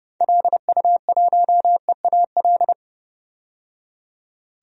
Morse code, heard as a keyed tone, about 30 wpm, 700 Hz. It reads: LU1EAL
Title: DXCC-100 Call Signs - 6 Characters - 30wpm